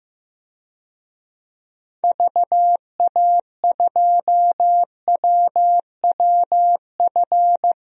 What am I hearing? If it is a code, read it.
Morse code, heard as a keyed tone, about 15 wpm, 700 Hz. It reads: VA2WWF
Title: DXCC-100 Call Signs - 6 Characters - 15wpm